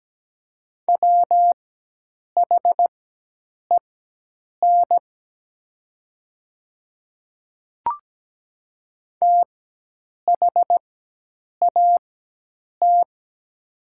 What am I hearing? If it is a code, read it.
Morse code, heard as a keyed tone, about 17 wpm, spaced as for 8 wpm, 700 Hz. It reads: WHEN THAT
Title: 4x Character Spacing - Top 100 Words - 17wpm